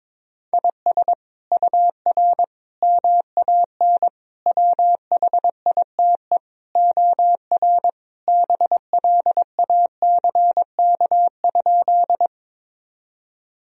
Morse code, heard as a keyed tone, about 22 wpm, 700 Hz. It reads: IS UR MAN WHITE OR BLACK?
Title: Sentences from Top 400 Words and Abbreviations 22wpm